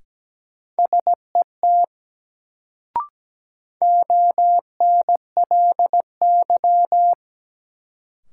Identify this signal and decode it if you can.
Morse code, heard as a keyed tone, about 17 wpm, 700 Hz. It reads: SET ONLY
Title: Sentences from Top 200 Words - New Word Drill - 17wpm